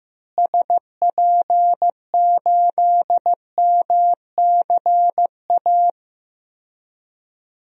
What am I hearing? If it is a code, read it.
Morse code, heard as a keyed tone, about 15 wpm, 700 Hz. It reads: SP8MCA
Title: DXCC-100 Call Signs - 6 Characters - Repeated 3x - 15wpm